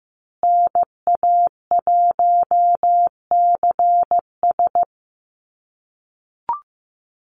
Morse code, heard as a keyed tone, about 15 wpm, 700 Hz. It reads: NA1CS
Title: USA Call Signs Repeated 2x 15wpm